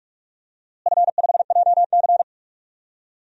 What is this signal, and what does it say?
Morse code, heard as a keyed tone, about 41 wpm, 700 Hz. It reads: F5JC
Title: DXCC-100 Call Signs - 4 Characters - Repeated 3x - 45wpm